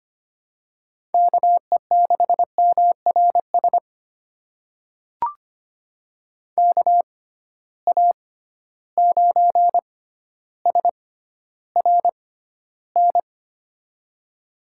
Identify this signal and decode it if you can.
Morse code, heard as a keyed tone, about 25 wpm, spaced as for 13 wpm, 700 Hz. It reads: KE6MRS KA9SRN
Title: USA Call Signs - 6 Characters - 6x Character Spacing - 25wpm